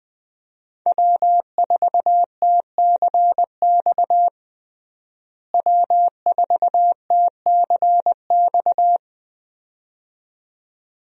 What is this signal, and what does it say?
Morse code, heard as a keyed tone, about 20 wpm, 700 Hz. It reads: W4TCX W4TCX
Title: USA Call Signs - 5 Characters - Repeated 2x - 20wpm